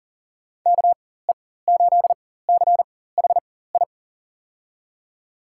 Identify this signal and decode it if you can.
Morse code, heard as a keyed tone, about 41 wpm, spaced as for 19 wpm, 700 Hz. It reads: KE8CHI
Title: USA Call Signs - 4x Character Spacing - 40wpm